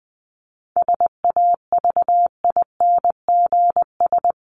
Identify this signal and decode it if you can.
Morse code, heard as a keyed tone, about 20 wpm, 700 Hz. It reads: SAVINGS
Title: Top 1001-1250 Words 20wpm